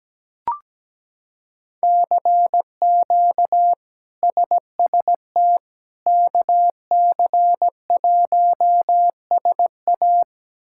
Morse code, heard as a keyed tone, about 17 wpm, 700 Hz. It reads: CQ SST KC1SA
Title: K1USN SST QSO Practice - 17wpm